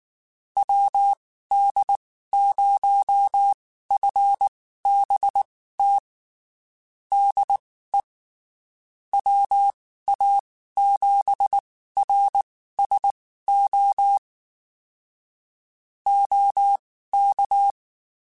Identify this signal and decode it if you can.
Morse code, heard as a keyed tone, about 19 wpm, spaced as for 14 wpm, 800 Hz. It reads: WD0FBT DE WA7RSO OK
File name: QSOs Spaced